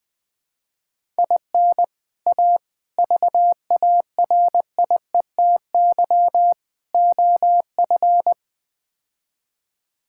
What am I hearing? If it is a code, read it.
Morse code, heard as a keyed tone, about 21 wpm, 700 Hz. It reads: IN A VARIETY OF
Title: Build A Phrase - 4-Word Phrases - 20wpm